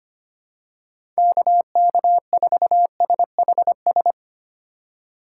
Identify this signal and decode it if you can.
Morse code, heard as a keyed tone, about 25 wpm, 700 Hz. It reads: KK4SHS
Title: USA Call Signs - 6 Characters - 4x Character Spacing - 25wpm